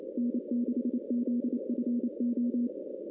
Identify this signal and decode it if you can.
Morse code, heard as a keyed tone, about 27 wpm, 250 Hz. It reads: N6ZFO
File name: 2016 arrl dx cw / 2016-ARRL-DX-CW-K5TR- - 0-25